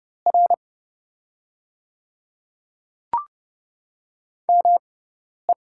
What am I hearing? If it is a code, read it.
Morse code, heard as a keyed tone, about 29 wpm, spaced as for 10 wpm, 700 Hz. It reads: R ME
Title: 6x Character Spacing - Top 300 Words - 30wpm